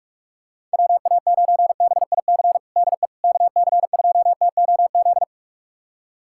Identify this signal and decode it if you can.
Morse code, heard as a keyed tone, about 46 wpm, 700 Hz. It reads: WA9BIC DE KC2TOZ
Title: USA Callsign DE Callsign 45wpm